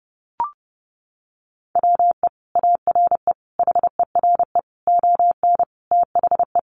Morse code, heard as a keyed tone, about 30 wpm, 700 Hz. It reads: WE ARE HERE ON THE
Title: Sentences from Top 1400 Words 30wpm